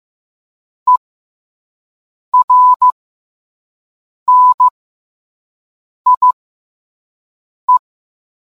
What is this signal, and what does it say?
Morse code, heard as a keyed tone, about 15 wpm, spaced as for 5 wpm, 1000 Hz. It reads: ERNIE